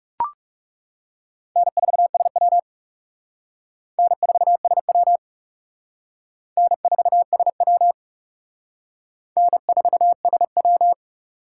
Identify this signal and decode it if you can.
Morse code, heard as a keyed tone, about 36 wpm, 700 Hz. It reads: N4SW N4SW N4SW N4SW
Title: Speed Racing - USA Call Signs 30wpm